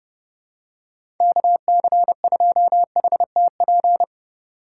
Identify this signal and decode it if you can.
Morse code, heard as a keyed tone, about 30 wpm, 700 Hz. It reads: KC2HTP
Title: USA Call Signs - 6 Characters - 3x Character Spacing - 30wpm